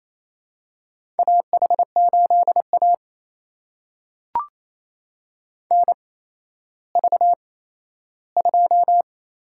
Morse code, heard as a keyed tone, about 28 wpm, spaced as for 14 wpm, 700 Hz. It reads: AH8A NV2
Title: USA Call Signs - 4 Characters - 8x Character Spacing - 28wpm